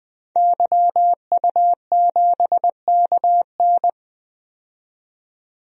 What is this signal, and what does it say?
Morse code, heard as a keyed tone, about 20 wpm, 700 Hz. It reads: YU7KN
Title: DXCC-100 Call Signs - 5 Characters - 20wpm